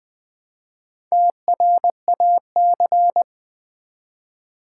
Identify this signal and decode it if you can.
Morse code, heard as a keyed tone, about 20 wpm, 700 Hz. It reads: TRAC
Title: Build A Word - 1 Letter At A Time - 5-Letter Words - 20wpm